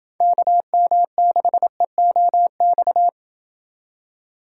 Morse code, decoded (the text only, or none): KM6EOX